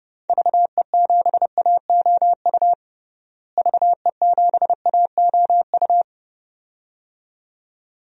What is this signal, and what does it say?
Morse code, heard as a keyed tone, about 32 wpm, 700 Hz. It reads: VE7AOU VE7AOU
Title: DXCC-100 Call Signs - 6 Characters - Repeated 2x - 30wpm